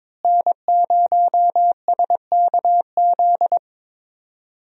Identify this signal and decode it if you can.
Morse code, heard as a keyed tone, about 22 wpm, 700 Hz. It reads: N0SKZ
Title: USA Call Signs - 5 Characters - Repeated 3x - 22wpm